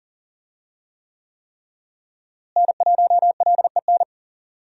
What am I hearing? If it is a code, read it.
Morse code, heard as a keyed tone, about 40 wpm, 700 Hz. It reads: N1LEN
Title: USA Call Signs - 5 Characters - Repeated 3x - 40wpm